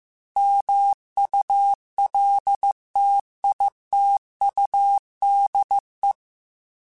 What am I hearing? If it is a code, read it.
Morse code, heard as a keyed tone, about 15 wpm, 800 Hz. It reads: MULTITUDE
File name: Technical Subjects / Technical Subjects-mp3s / OR Gate @13